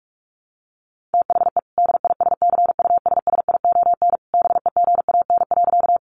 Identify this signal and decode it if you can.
Morse code, heard as a keyed tone, about 44 wpm, 700 Hz. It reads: THE DISCUSSION BEGAN.
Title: Sentences from Top 1100 Words 45wpm